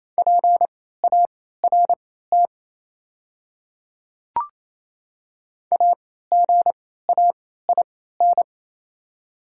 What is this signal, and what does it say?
Morse code, heard as a keyed tone, about 28 wpm, spaced as for 16 wpm, 700 Hz. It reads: PART AGAIN
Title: 3x Character Spacing - Top 300 Words - 28wpm